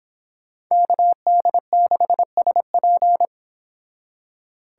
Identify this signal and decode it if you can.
Morse code, heard as a keyed tone, about 30 wpm, 700 Hz. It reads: KD6SP